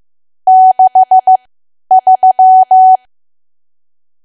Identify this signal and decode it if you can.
Morse code, heard as a keyed tone, about 15 wpm, spaced as for 10 wpm, 750 Hz. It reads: 63
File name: Archive / 10 WPM / 10WPM